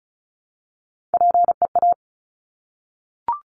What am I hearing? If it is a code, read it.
Morse code, heard as a keyed tone, about 36 wpm, 700 Hz. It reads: PEA
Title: Top Three Letter Combinations in Words 35wpm